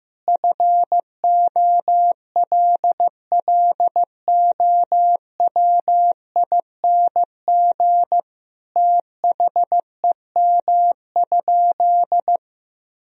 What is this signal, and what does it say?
Morse code, heard as a keyed tone, about 15 wpm, 700 Hz. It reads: FOLLOWING THEM?
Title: Sentences from Top 1000 Words - Review Part 4 - 15wpm